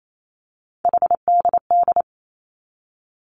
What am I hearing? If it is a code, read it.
Morse code, heard as a keyed tone, about 28 wpm, 700 Hz. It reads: HDD
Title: Three Challenging Letter-Number Combinations 28wpm